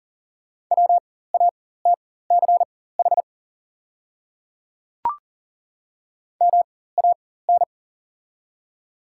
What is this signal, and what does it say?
Morse code, heard as a keyed tone, about 40 wpm, spaced as for 19 wpm, 700 Hz. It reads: WATCH MAN
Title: 4x Character Spacing - Top 300 Words - 40wpm